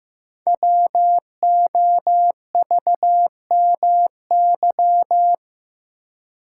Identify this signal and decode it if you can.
Morse code, heard as a keyed tone, about 15 wpm, 700 Hz. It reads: WOVMY
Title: Code Groups - 5 Letters - 15wpm